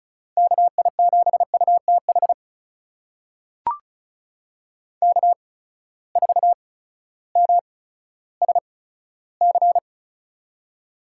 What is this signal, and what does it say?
Morse code, heard as a keyed tone, about 36 wpm, spaced as for 16 wpm, 700 Hz. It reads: KI7UTH K4MSC
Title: USA Call Signs - 8x Character Spacing - 35wpm